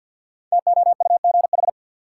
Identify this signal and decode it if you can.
Morse code, heard as a keyed tone, about 47 wpm, 700 Hz. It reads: TOUGH
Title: Sentences from Top 1200 Words - New Word Drill - 50wpm